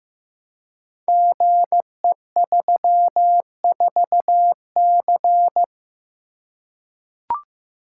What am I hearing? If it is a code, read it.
Morse code, heard as a keyed tone, about 15 wpm, 700 Hz. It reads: GE34C
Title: Code Groups - 5 Letter:Numbers - 15wpm